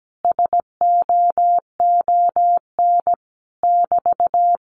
Morse code, heard as a keyed tone, about 17 wpm, 700 Hz. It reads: SOON <BT>
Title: Common QSO Phrases Encore 17wpm